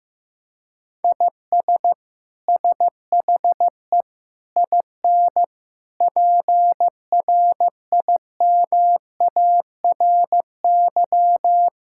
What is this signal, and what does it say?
Morse code, heard as a keyed tone, about 15 wpm, 700 Hz. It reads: IS SHE IN PRIMARY
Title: Sentences from Top 1000 Words - Review Part 5 - 15wpm